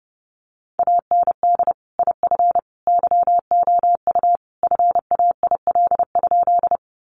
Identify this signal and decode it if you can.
Morse code, heard as a keyed tone, about 28 wpm, 700 Hz. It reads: AND IF YOU FAIL?
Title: Sentences from Top 1100 Words 30wpm